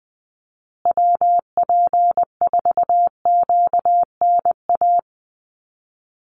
Morse code, heard as a keyed tone, about 20 wpm, 700 Hz. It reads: WP4QNA
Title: USA Call Signs Repeated 3x 20wpm